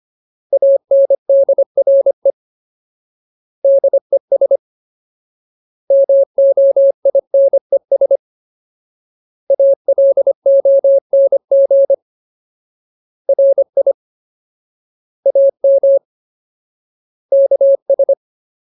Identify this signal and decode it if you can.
Morse code, heard as a keyed tone, about 25 wpm, 550 Hz. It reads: ANDRE DES MOINES ALONG RI AM KS
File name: Ultimate Word List / 25WPM Ultimate